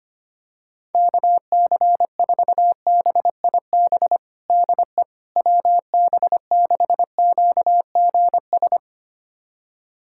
Morse code, heard as a keyed tone, about 25 wpm, 700 Hz. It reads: KC4BIB DE WB6QGS
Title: USA Callsign DE Callsign - 6 Characters - 25wpm